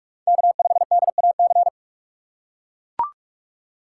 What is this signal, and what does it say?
Morse code, heard as a keyed tone, about 41 wpm, 700 Hz. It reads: K5DAC
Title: USA Call Signs - 4x Character Spacing - 45wpm